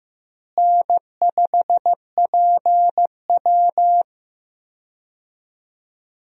Morse code, heard as a keyed tone, about 15 wpm, 700 Hz. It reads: N5PW
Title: USA Call Signs - 4 Characters - Repeated 2x - 15wpm